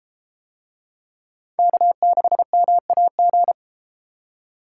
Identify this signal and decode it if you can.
Morse code, heard as a keyed tone, about 32 wpm, 700 Hz. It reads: K6MAG